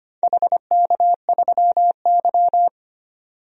Morse code, heard as a keyed tone, about 25 wpm, 700 Hz. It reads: HK3Y
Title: DXCC-100 Call Signs - 4 Characters - 25wpm